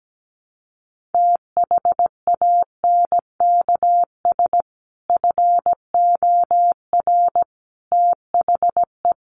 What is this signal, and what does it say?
Morse code, heard as a keyed tone, about 17 wpm, 700 Hz. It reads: THANKS FOR THE
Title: Sentences from Top 1500 Words 17wpm